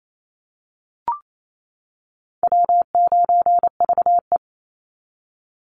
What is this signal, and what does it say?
Morse code, heard as a keyed tone, about 28 wpm, 700 Hz. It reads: W9VE
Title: Easy Call Signs - 28wpm